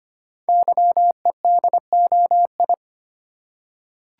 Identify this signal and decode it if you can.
Morse code, heard as a keyed tone, about 25 wpm, 700 Hz. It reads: YEDOI
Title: Code Groups - 5 Letters - 25wpm